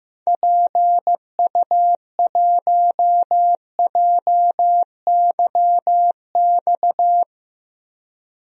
Morse code, heard as a keyed tone, about 15 wpm, 700 Hz. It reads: PU1JYX
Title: DXCC-100 Call Signs - 6 Characters - 15wpm